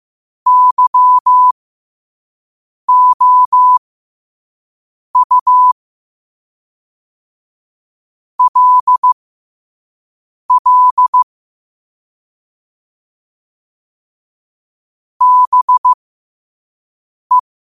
Morse code, heard as a keyed tone, about 15 wpm, spaced as for 5 wpm, 1000 Hz. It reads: YOU LL BE
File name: qotdfiles / 5wpm